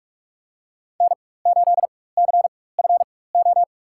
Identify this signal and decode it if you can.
Morse code, heard as a keyed tone, about 41 wpm, spaced as for 20 wpm, 700 Hz. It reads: N8CFO
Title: USA Call Signs - 5 Characters - 4x Character Spacing - 45wpm